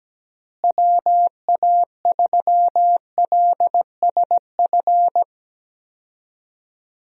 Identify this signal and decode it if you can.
Morse code, heard as a keyed tone, about 17 wpm, 700 Hz. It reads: WA3LSF
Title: USA Call Signs - 6 Characters - Repeated 2x - 17wpm